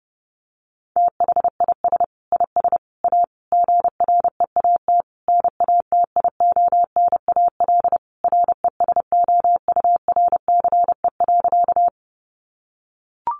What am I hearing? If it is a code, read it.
Morse code, heard as a keyed tone, about 30 wpm, 700 Hz. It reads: THIS IS A GREAT NATIONAL RESOURCE.
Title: Sentences from Top 1300 Words 30wpm